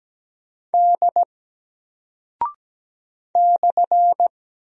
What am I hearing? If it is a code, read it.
Morse code, heard as a keyed tone, about 17 wpm, 700 Hz. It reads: D /
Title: Single Character (Letter:Number:Punctuation:Prosign) 17wpm